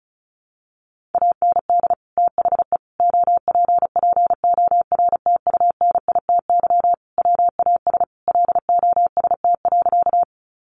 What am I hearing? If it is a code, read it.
Morse code, heard as a keyed tone, about 34 wpm, 700 Hz. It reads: AND THE OPPORTUNITY WAS LOST.
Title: Sentences from Top 1400 Words 35wpm